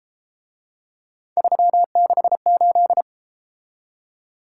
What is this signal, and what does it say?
Morse code, heard as a keyed tone, about 31 wpm, 700 Hz. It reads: 368